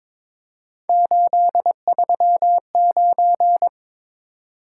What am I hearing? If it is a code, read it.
Morse code, heard as a keyed tone, about 22 wpm, 700 Hz. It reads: 839
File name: practice / 22wpm / Three Letter-Number Combinations - 22wpm